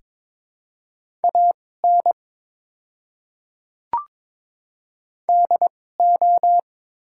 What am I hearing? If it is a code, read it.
Morse code, heard as a keyed tone, about 22 wpm, spaced as for 16 wpm, 700 Hz. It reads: AN DO
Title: 2x Character Spacing - Top 100 Words - 22wpm